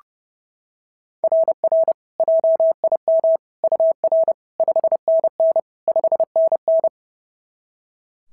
Morse code, heard as a keyed tone, about 30 wpm, 660 Hz. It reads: RR JIM UR 5NN 5NN
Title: SOTA QSO Practice 30wpm